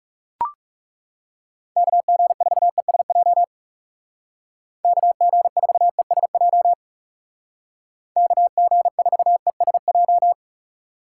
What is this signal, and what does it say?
Morse code, heard as a keyed tone, about 36 wpm, 700 Hz. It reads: KG4ESJ KG4ESJ KG4ESJ
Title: Speed Racing - USA Call Signs 30wpm